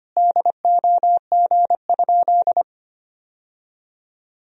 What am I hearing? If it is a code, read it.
Morse code, heard as a keyed tone, about 25 wpm, 700 Hz. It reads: DOG?
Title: Sentences from Top 1000 Words - Review Part 2 - 25wpm